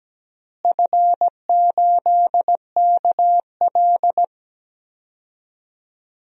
Morse code, heard as a keyed tone, about 17 wpm, 700 Hz. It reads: F8KL
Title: DXCC-100 Call Signs - 4 Characters - Repeated 3x - 17wpm